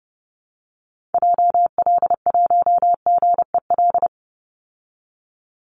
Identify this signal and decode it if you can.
Morse code, heard as a keyed tone, about 30 wpm, 700 Hz. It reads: JL1GEL
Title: Call Signs Repeated 3x - 30wpm